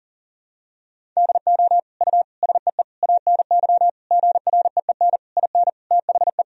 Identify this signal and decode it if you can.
Morse code, heard as a keyed tone, about 40 wpm, 700 Hz. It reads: DO U SEE ANY GREEN IN THE
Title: Sentences from Top 500 Words and Abbreviations 40wpm